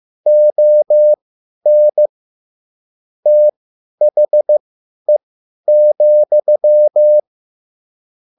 Morse code, heard as a keyed tone, about 15 wpm, spaced as for 10 wpm, 600 Hz. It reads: ON THE,
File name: QSL / CW lesson